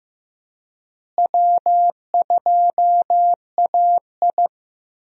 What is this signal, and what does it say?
Morse code, heard as a keyed tone, about 15 wpm, 700 Hz. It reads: W2AI
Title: USA Call Signs - 4 Characters - Repeated 3x - 15wpm